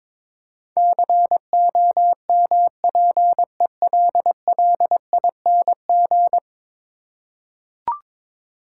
Morse code, heard as a keyed tone, about 21 wpm, 700 Hz. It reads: COMPELLING
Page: morse-code-ninja-practice-22wpm